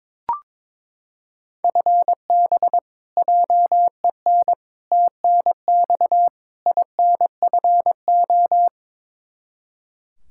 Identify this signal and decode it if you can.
Morse code, heard as a keyed tone, about 21 wpm, 700 Hz. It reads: FB JEN TNX INFO
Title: Practice QSO - Exchange 3 - More Info - 22wpm